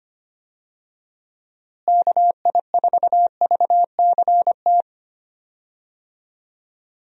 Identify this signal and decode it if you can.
Morse code, heard as a keyed tone, about 25 wpm, 700 Hz. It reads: KI4VCT
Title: USA Call Signs - 6 Characters - Repeated 3x - 25wpm